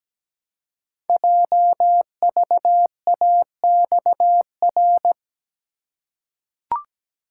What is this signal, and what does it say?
Morse code, heard as a keyed tone, about 17 wpm, 700 Hz. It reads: JVAXR
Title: Code Groups - 5 Letter:Numbers - 17wpm